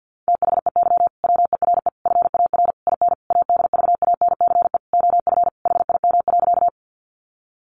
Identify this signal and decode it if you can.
Morse code, heard as a keyed tone, about 50 wpm, 700 Hz. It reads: THEY WERE FAR IN ADVANCE OF HIM.
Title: Sentences from Top 1400 Words 50wpm